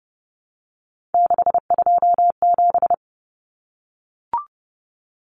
Morse code, heard as a keyed tone, about 30 wpm, 700 Hz. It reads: 627
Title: Common Cut Numbers - 3-Digit 30wpm